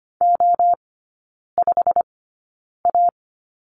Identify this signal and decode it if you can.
Morse code, heard as a keyed tone, about 25 wpm, spaced as for 9 wpm, 690 Hz. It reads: O5A
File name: Lesson 15e - Number 5 - Three Characters 25wpm